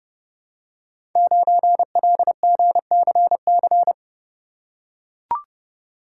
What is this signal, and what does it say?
Morse code, heard as a keyed tone, about 30 wpm, 700 Hz. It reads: 9LGCC
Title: Code Groups - 5 Letter:Numbers - 30wpm